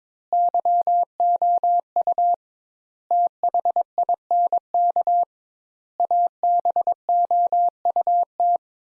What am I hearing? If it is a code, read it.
Morse code, heard as a keyed tone, about 22 wpm, 700 Hz. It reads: YOU THINK ABOUT
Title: Sentences Top 200 Words - 22wpm